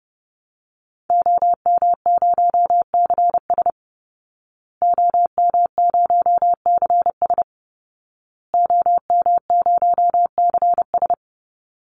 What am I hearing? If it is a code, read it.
Morse code, heard as a keyed tone, about 30 wpm, 700 Hz. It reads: OM0CS OM0CS OM0CS
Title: Call Signs Repeated 2x - 30wpm